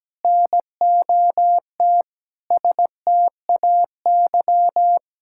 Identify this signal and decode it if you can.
Morse code, heard as a keyed tone, about 17 wpm, 700 Hz. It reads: NOT STAY
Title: Sentences from Top 1600 Words 17wpm